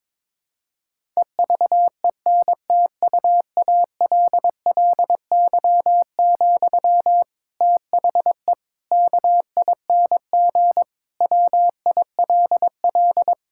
Morse code, heard as a keyed tone, about 21 wpm, 700 Hz. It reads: EVENTUALLY, THE KING WILL
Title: Sentences from Top 1000 Words - Review Part 6 - 22wpm